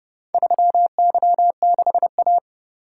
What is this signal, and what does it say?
Morse code, heard as a keyed tone, about 29 wpm, 700 Hz. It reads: 3Y6A